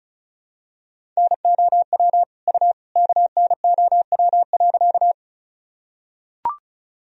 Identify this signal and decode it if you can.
Morse code, heard as a keyed tone, about 36 wpm, 700 Hz. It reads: NOW U KNOW.
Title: Sentences from Top 100 Words and Abbreviations 35wpm